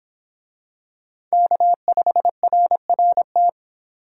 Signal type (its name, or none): Morse code